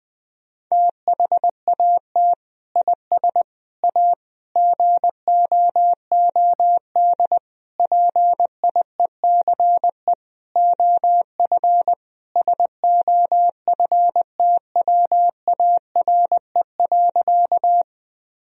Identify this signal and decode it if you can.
Morse code, heard as a keyed tone, about 20 wpm, 700 Hz. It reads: THAT IS A GOOD PIECE OF SOFTWARE.
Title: Sentences from Top 1000 Words - Review Part 3 - 20wpm